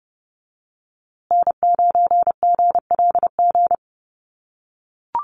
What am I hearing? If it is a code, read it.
Morse code, heard as a keyed tone, about 30 wpm, 700 Hz. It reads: N9GLG
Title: USA Call Signs Repeated 2x 30wpm